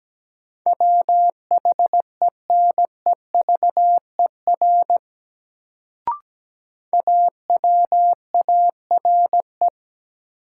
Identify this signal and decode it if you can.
Morse code, heard as a keyed tone, about 17 wpm, 700 Hz. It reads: WHENEVER AWARE
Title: Sentences from Top 1200 Words - New Word Drill - 17wpm